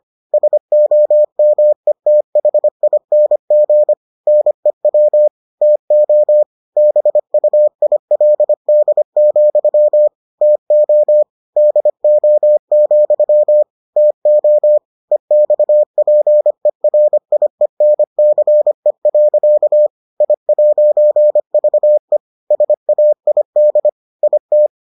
Morse code, heard as a keyed tone, about 25 wpm, 600 Hz. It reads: SOMETHING NEW TO BUILD, TO DO, TO EXPERIENCE. I'VE SAID IT